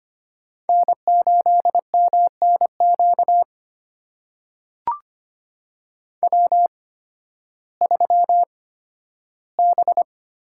Morse code, heard as a keyed tone, about 25 wpm, spaced as for 13 wpm, 700 Hz. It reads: N8MNQ W3B
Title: USA Call Signs - 8x Character Spacing - 25wpm